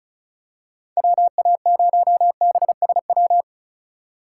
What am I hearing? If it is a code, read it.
Morse code, heard as a keyed tone, about 35 wpm, 700 Hz. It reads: WA0BSW
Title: USA Call Signs - 6 Characters - 3x Character Spacing - 35wpm